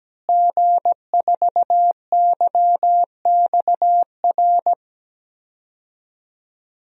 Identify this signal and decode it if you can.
Morse code, heard as a keyed tone, about 17 wpm, 700 Hz. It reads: G4YXR
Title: DXCC-100 Call Signs - 5 Characters - Repeated 2x - 17wpm